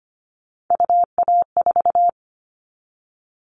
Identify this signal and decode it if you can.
Morse code, heard as a keyed tone, about 25 wpm, 700 Hz. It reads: UA4
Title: Callsign Prefixes for Top 100 DXCC - 25wpm